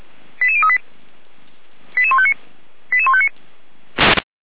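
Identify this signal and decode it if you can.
Five-tone selective call (ZVEI), heard as quick runs of five tones, tones of about 70 ms. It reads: 88038 88159 88159
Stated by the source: ZVEI five-tone selective call received by SDR.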